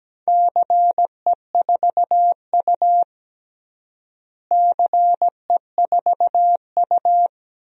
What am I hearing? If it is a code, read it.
Morse code, heard as a keyed tone, about 17 wpm, 700 Hz. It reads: CE4U CE4U
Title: DXCC-100 Call Signs - 4 Characters - Repeated 2x - 17wpm